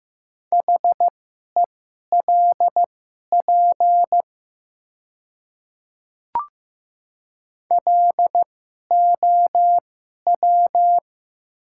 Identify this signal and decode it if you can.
Morse code, heard as a keyed tone, about 15 wpm, spaced as for 11 wpm, 700 Hz. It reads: HELP LOW
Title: 2x Character Spacing - Top 300 Words - 15wpm